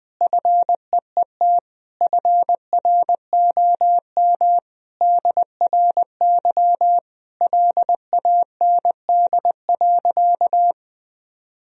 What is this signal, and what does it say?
Morse code, heard as a keyed tone, about 21 wpm, 700 Hz. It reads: FEET FROM DRY LAND.
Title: Sentences from Top 1000 Words - Review Part 3 - 20wpm